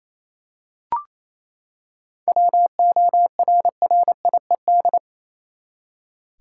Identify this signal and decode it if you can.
Morse code, heard as a keyed tone, about 28 wpm, 700 Hz. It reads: WORRIED
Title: Seven Letter Words - 28wpm